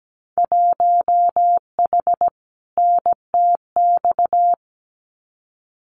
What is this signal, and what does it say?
Morse code, heard as a keyed tone, about 17 wpm, 700 Hz. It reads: 1H NTX
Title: Winter Field Day Exchanges 17wpm